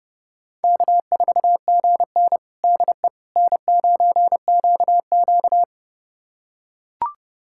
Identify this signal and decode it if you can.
Morse code, heard as a keyed tone, about 32 wpm, 700 Hz. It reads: K4GN DE N9QQ
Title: USA Callsign DE Callsign - 4 Characters - 30wpm